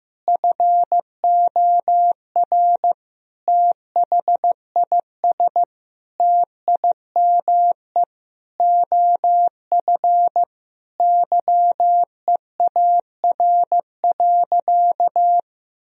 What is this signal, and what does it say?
Morse code, heard as a keyed tone, about 15 wpm, 700 Hz. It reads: FOR THIS TIME OF YEAR.
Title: Sentences from Top 1000 Words - Review Part 3 - 15wpm